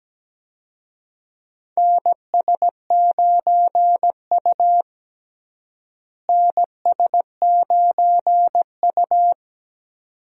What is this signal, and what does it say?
Morse code, heard as a keyed tone, about 17 wpm, 700 Hz. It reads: NS9U NS9U
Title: USA Call Signs - 4 Characters - Repeated 2x - 17wpm